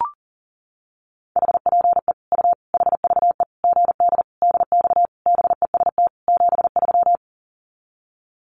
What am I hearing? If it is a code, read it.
Morse code, heard as a keyed tone, about 40 wpm, 700 Hz. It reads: HPE U HVE GD DX BEST 73
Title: Common QSO Phrases 40wpm